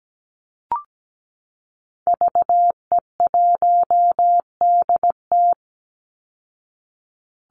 Morse code, heard as a keyed tone, about 17 wpm, 700 Hz. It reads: VE1DT
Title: Call Signs Repeated 3x - 17wpm